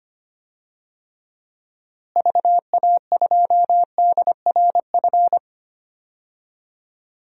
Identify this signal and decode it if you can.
Morse code, heard as a keyed tone, about 25 wpm, 700 Hz. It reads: VA2DRF
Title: DXCC-100 Call Signs - 6 Characters - Repeated 3x - 25wpm